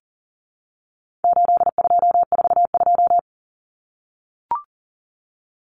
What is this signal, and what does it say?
Morse code, heard as a keyed tone, about 40 wpm, 700 Hz. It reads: 8242
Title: Common Cut Numbers - 4-Digit 40wpm